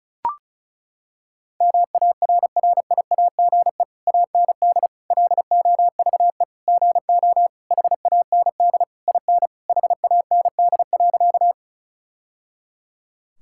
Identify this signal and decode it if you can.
Morse code, heard as a keyed tone, about 36 wpm, 700 Hz. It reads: MARRIAGE AND LOVE GO HAND IN HAND.
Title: Sentences from Top 1000 Words - Review Part 5 - 35wpm